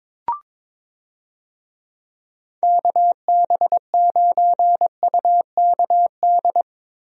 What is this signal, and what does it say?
Morse code, heard as a keyed tone, about 22 wpm, 700 Hz. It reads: KB9UKD
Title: USA Call Signs - 6 Characters - Repeated 3x - 22wpm